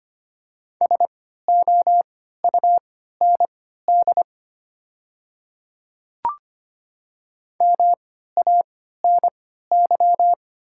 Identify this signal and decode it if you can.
Morse code, heard as a keyed tone, about 25 wpm, spaced as for 14 wpm, 700 Hz. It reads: SOUND MANY
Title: 3x Character Spacing - Top 100 Words - 25wpm